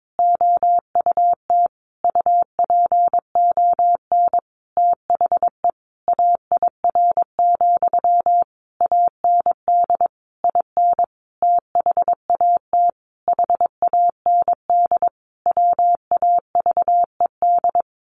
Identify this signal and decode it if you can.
Morse code, heard as a keyed tone, about 22 wpm, 700 Hz. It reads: OUT UPON THE AIR, AND IN THAT HAND WAVED
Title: A Sound Of Thunder 22wpm